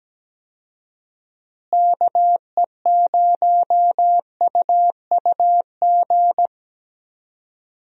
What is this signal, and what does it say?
Morse code, heard as a keyed tone, about 17 wpm, 700 Hz. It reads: KE0UUG
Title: USA Call Signs - 6 Characters - Repeated 2x - 17wpm